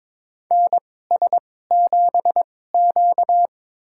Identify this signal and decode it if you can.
Morse code, heard as a keyed tone, about 22 wpm, spaced as for 16 wpm, 700 Hz. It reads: NS7Q
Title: USA Call Signs - 4 Characters - 2x Character Spacing - 22wpm